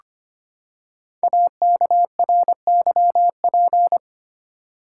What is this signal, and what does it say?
Morse code, heard as a keyed tone, about 25 wpm, 700 Hz. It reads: AKRYP
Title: Code Groups - 5 Letters - 25wpm